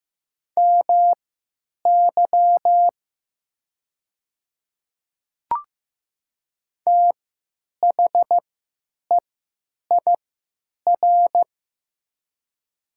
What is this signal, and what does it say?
Morse code, heard as a keyed tone, about 15 wpm, spaced as for 9 wpm, 700 Hz. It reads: MY THEIR
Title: 3x Character Spacing - Top 100 Words - 15wpm